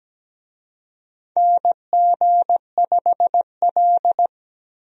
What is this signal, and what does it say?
Morse code, heard as a keyed tone, about 17 wpm, 700 Hz. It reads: NG5L
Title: USA Call Signs - 4 Characters - Repeated 2x - 17wpm